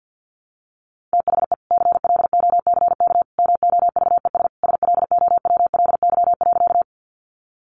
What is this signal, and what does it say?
Morse code, heard as a keyed tone, about 50 wpm, 700 Hz. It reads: THE CLOCK MOVES SLOWLY.
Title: Easy Phrases 50wpm